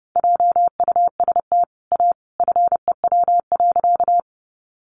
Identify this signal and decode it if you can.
Morse code, heard as a keyed tone, about 30 wpm, 700 Hz. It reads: JUST A FEW.
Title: Sentences from Top 300 Words 30wpm